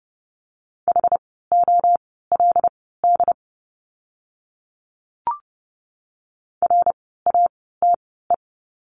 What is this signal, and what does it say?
Morse code, heard as a keyed tone, about 30 wpm, spaced as for 17 wpm, 700 Hz. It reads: HOLD RATE
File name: Lesson 17f - Letter u - Words 30wpm